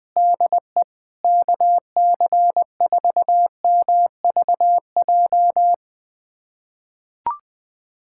Morse code, heard as a keyed tone, about 20 wpm, 700 Hz. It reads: DE KC4MVJ
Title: USA Callsign DE Callsign - 6 Characters - 20wpm